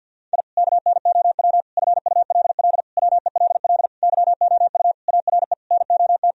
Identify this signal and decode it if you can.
Morse code, heard as a keyed tone, about 52 wpm, 700 Hz. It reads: I KNOW FULL WELL YOU ARE NOT
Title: Sentences from Top 1000 Words - Review Part 3 - 50wpm